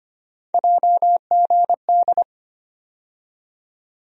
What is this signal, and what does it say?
Morse code, heard as a keyed tone, about 25 wpm, 700 Hz. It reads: JGD
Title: USA Call Sign Postfixes - 3 Letters 25wpm